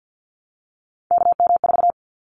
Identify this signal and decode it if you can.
Morse code, heard as a keyed tone, about 50 wpm, 700 Hz. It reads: KM4
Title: Callsign Prefixes for Top 25 DXCC - Repeated 3x - 50wpm